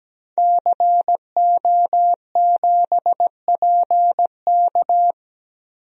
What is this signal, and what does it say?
Morse code, heard as a keyed tone, about 17 wpm, 700 Hz. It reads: CO7PK
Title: DXCC-100 Call Signs - 5 Characters - Repeated 2x - 17wpm